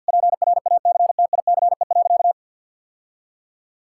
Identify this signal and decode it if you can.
Morse code, heard as a keyed tone, about 52 wpm, 700 Hz. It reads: PRACTICE.
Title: Sentences from Top 1000 Words - Review Part 6 - 50wpm